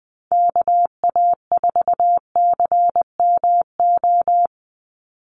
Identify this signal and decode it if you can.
Morse code, heard as a keyed tone, about 20 wpm, 700 Hz. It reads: KA4CMO
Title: USA Call Signs 20wpm